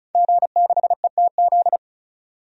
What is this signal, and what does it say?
Morse code, heard as a keyed tone, about 34 wpm, 700 Hz. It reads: G6ETZ
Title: DXCC-100 Call Signs - 5 Characters - 35wpm